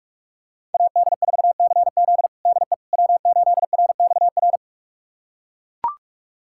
Morse code, heard as a keyed tone, about 41 wpm, 700 Hz. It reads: AD4CZ DE W8RXR
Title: USA Callsign DE Callsign - 5 Characters - 45wpm